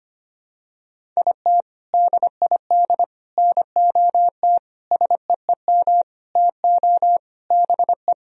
Morse code, heard as a keyed tone, about 25 wpm, 700 Hz. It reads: IT DID NOT SEEM TO BE
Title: Sentences from Top 1000 Words - Review Part 1 - 25wpm